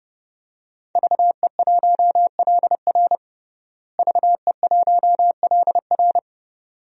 Morse code, heard as a keyed tone, about 30 wpm, 700 Hz. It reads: VE1LR VE1LR
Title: DXCC-100 Call Signs - 5 Characters - Repeated 2x - 30wpm